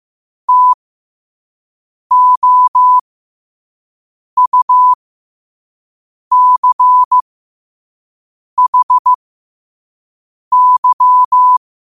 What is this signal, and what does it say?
Morse code, heard as a keyed tone, about 15 wpm, spaced as for 5 wpm, 1000 Hz. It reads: TOUCHY
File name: qotdfiles / 5wpm